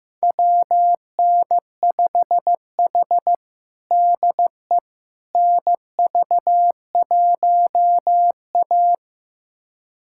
Morse code, heard as a keyed tone, about 15 wpm, 700 Hz. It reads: WN5H DE NV1A
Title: USA Callsign DE Callsign - 4 Characters - 15wpm